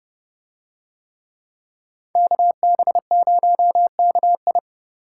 Morse code, heard as a keyed tone, about 29 wpm, 700 Hz. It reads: KB0KI